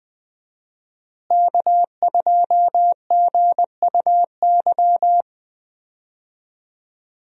Morse code, heard as a keyed tone, about 20 wpm, 700 Hz. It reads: K2GUY
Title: USA Call Signs - 5 Characters - Repeated 3x - 20wpm